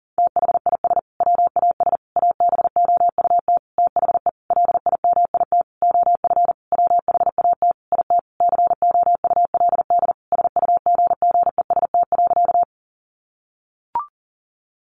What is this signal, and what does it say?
Morse code, heard as a keyed tone, about 40 wpm, 700 Hz. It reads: THIS WAS ABOUT THE LIMIT OF WHAT IT COULD SUGGEST.
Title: Sentences from Top 1500 Words 40wpm